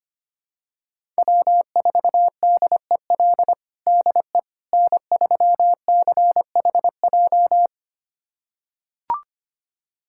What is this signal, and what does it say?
Morse code, heard as a keyed tone, about 25 wpm, 700 Hz. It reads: W4DEL DE N3CHJ
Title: USA Callsign DE Callsign - 5 Characters - 25wpm